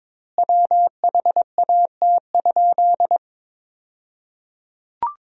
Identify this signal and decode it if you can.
Morse code, heard as a keyed tone, about 23 wpm, 700 Hz. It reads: WHAT?
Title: 2x Word-Spacing - Sentences from Top 500 Words - 22wpm